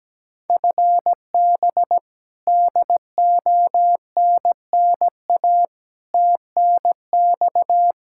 Morse code, heard as a keyed tone, about 17 wpm, 700 Hz. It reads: FB DONNA TNX
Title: Practice QSO - Exchange 3 - More Info - 17wpm